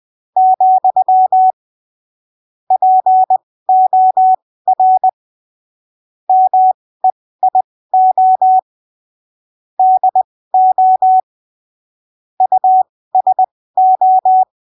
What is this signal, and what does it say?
Morse code, heard as a keyed tone, about 20 wpm, spaced as for 15 wpm, 750 Hz. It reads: , POR MEIO DO USO
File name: arquivos / texto / texto 18 20ppm